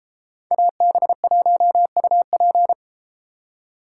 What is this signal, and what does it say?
Morse code, heard as a keyed tone, about 33 wpm, 700 Hz. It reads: AB1UP